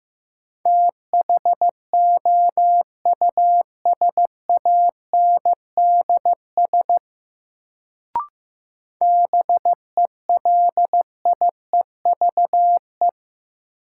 Morse code, heard as a keyed tone, about 15 wpm, 700 Hz. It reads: THOUSANDS BELIEVE
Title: Sets of 4 Words - New Word Drill - 15wpm